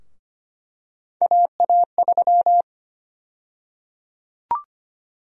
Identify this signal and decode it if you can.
Morse code, heard as a keyed tone, about 25 wpm, 700 Hz. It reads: AA3
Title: USA Call Sign Prefixes - 25wpm